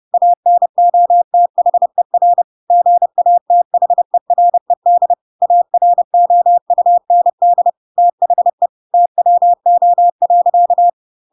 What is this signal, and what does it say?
Morse code, heard as a keyed tone, about 30 wpm, 700 Hz. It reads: ANOTHER GATHERED AROUND THE TWO.
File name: Pinocchio / Pinocchio ch3 30 30 ik1hge